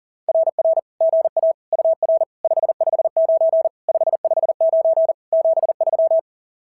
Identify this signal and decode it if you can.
Morse code, heard as a keyed tone, about 40 wpm, 650 Hz. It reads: RR GA UR 559 559 73
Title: SOTA QSO Practice 40wpm